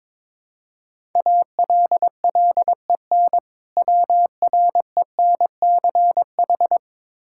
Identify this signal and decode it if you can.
Morse code, heard as a keyed tone, about 22 wpm, 700 Hz. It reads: ALLEN WRENCH
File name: Difficult Words with Hints - 22wpm